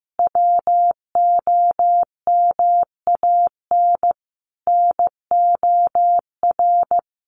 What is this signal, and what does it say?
Morse code, heard as a keyed tone, about 15 wpm, 700 Hz. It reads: WOMAN NOR
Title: Sentences from Top 1500 Words 15wpm